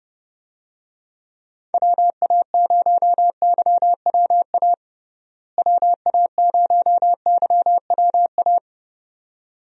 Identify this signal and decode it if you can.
Morse code, heard as a keyed tone, about 30 wpm, 700 Hz. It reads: WA0YWA WA0YWA
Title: USA Call Signs - 6 Characters - Repeated 2x - 30wpm